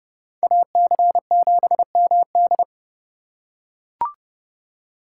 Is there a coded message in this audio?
Yes: Morse code from 0.4 to 2.6 s, heard as a keyed tone, about 30 wpm, 700 Hz. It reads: AC7MD